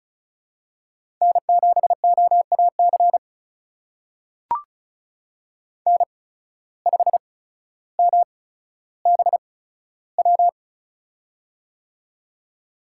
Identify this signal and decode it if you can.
Morse code, heard as a keyed tone, about 36 wpm, spaced as for 15 wpm, 700 Hz. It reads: N7OAC N5MBW
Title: USA Call Signs - 5 Characters - 8x Character Spacing - 35wpm